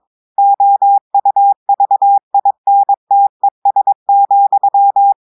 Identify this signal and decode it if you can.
Morse code, heard as a keyed tone, about 22 wpm, 800 Hz. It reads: OUVINTES,
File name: audios / portugais / portugais texte en clair 2 vitesse 22 mots minutes